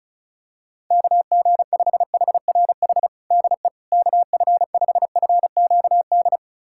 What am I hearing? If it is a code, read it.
Morse code, heard as a keyed tone, about 34 wpm, 700 Hz. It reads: KG5HRH DE KF5FQD
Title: USA Callsign DE Callsign - 6 Characters - 35wpm